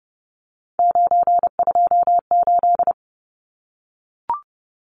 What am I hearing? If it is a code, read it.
Morse code, heard as a keyed tone, about 30 wpm, 700 Hz. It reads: 928
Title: Numbers - 3-Digit 30wpm